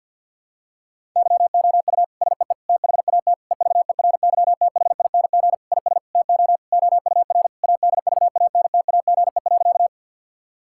Morse code, heard as a keyed tone, about 52 wpm, 700 Hz. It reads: YOU SEE THAT EVERYTHING IS TO OUR ADVANTAGE.
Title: Sentences from Top 1000 Words - Review Part 6 - 50wpm